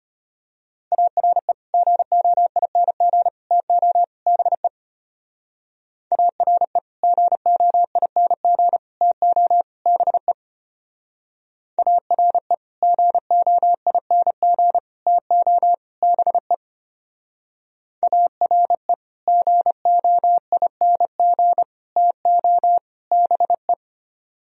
Morse code, heard as a keyed tone, about 30 wpm, 700 Hz. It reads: ARE GOING TO BE ARE GOING TO BE ARE GOING TO BE ARE GOING TO BE
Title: Speed-Racing - Sets of 4 Words 25wpm